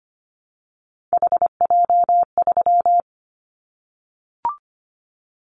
Morse code, heard as a keyed tone, about 25 wpm, 700 Hz. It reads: HJ3
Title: Callsign Prefixes for Top 100 DXCC - 25wpm